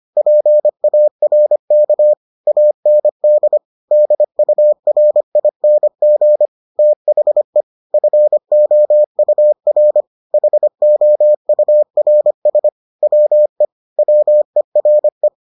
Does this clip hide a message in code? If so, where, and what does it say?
Yes: Morse code from 0.2 to 15.3 s, heard as a keyed tone, about 25 wpm, 600 Hz. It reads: PARK AND DURING THE FOUR HOURS WE WERE